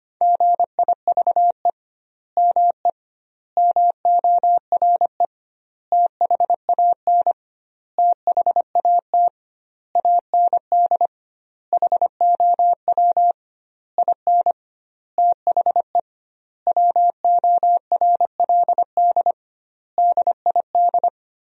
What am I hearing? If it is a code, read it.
Morse code, heard as a keyed tone, about 25 wpm, 700 Hz. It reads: GIVE ME MORE THAN THAT AND HOW IN THE WORLD DID
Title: 2x Word-Spacing - Sentences from Top 200 Words - 25wpm